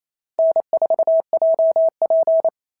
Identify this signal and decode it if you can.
Morse code, heard as a keyed tone, about 27 wpm, 690 Hz. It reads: N4JP
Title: SOTA QSO Practice 28wpm